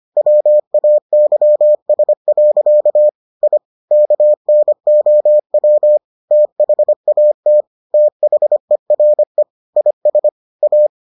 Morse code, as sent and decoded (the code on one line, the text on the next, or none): .-- .- -.-- ... .-.-.- / .. / -.- -. --- .-- / - .... .- - / - .... .-. / .. ... / .-
WAYS. I KNOW THAT THERE IS A